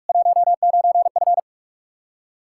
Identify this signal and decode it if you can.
Morse code, heard as a keyed tone, about 42 wpm, 700 Hz. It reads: 19F
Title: Practice QSO - Exchange 2 - Info - 45wpm